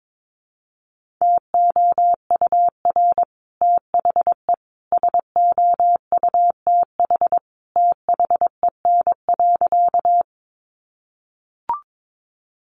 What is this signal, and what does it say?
Morse code, heard as a keyed tone, about 22 wpm, 700 Hz. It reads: TOUR THE SOUTH THEN.
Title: Sentences from Top 1400 Words 22wpm